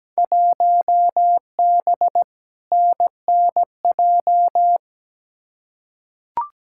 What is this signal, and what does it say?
Morse code, heard as a keyed tone, about 17 wpm, 700 Hz. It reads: 1B NNJ
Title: Field Day Exchanges Encore - 17wpm